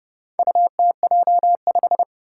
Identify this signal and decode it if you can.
Morse code, heard as a keyed tone, about 29 wpm, 700 Hz. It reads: UTJ5
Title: Four Challenging Letter-Number Combinations 30wpm